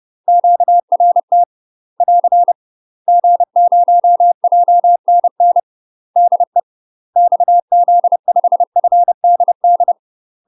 Morse code, heard as a keyed tone, about 30 wpm, 700 Hz. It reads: QRT <AR> G0JNN DE XZ5FDD
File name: UK MASTER QSOs TRAINING / 30WPM / UK MASTER QSOs 0008